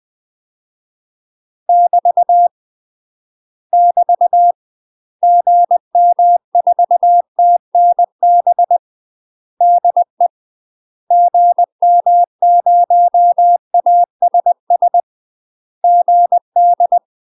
Morse code, heard as a keyed tone, about 20 wpm, 700 Hz. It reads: <BT> <BT> GM4TNB DE GM0ASS GD
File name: UK MASTER QSOs TRAINING / 20WPM / UK MASTER QSOs 0025